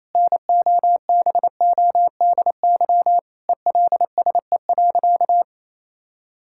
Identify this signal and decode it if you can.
Morse code, heard as a keyed tone, about 28 wpm, 700 Hz. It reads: NOBODY ELSE.
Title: Sentences from Top 1600 Words 28wpm